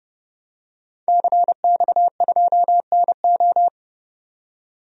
Morse code, heard as a keyed tone, about 30 wpm, 700 Hz. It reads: CX2NO
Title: DXCC-100 Call Signs - 5 Characters - 30wpm